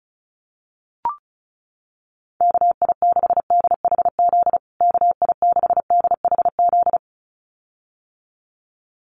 Morse code, heard as a keyed tone, about 34 wpm, 700 Hz. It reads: KI6DHZ KI6DHZ
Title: USA Call Signs Repeated 2x 35wpm